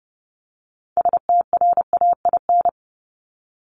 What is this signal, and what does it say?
Morse code, heard as a keyed tone, about 30 wpm, 700 Hz. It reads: STRAIN
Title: Top 1751-2000 Words 30wpm